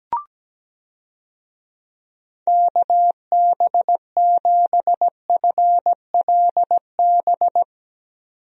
Morse code, heard as a keyed tone, about 17 wpm, 700 Hz. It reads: KB7FLB
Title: USA Call Signs - 6 Characters - 17wpm